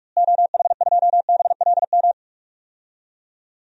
Morse code, heard as a keyed tone, about 44 wpm, 700 Hz. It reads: OH2BLM
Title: DXCC-100 Call Signs - 6 Characters - Repeated 2x - 45wpm